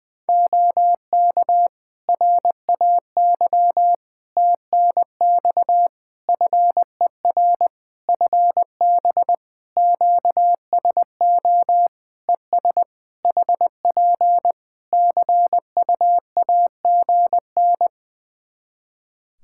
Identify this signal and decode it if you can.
Morse code, heard as a keyed tone, about 20 wpm, 700 Hz. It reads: OK RAY TNX FER FB QSO ES HP CUAGN
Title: Practice QSO - Exchange 4 - Ending - 20wpm